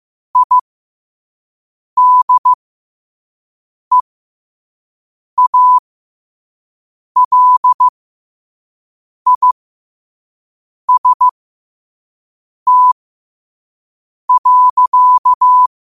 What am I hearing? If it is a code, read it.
Morse code, heard as a keyed tone, about 15 wpm, spaced as for 5 wpm, 1000 Hz. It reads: IDEALIST.
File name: qotdfiles / 5wpm